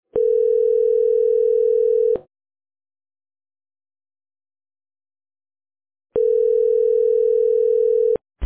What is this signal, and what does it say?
Ringback tone (North America), 440 + 480 Hz.